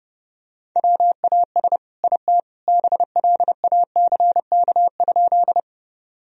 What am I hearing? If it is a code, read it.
Morse code, heard as a keyed tone, about 30 wpm, 700 Hz. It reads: WAS IT BLACK?
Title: Sentences from Top 1000 Words - Review Part 2 - 30wpm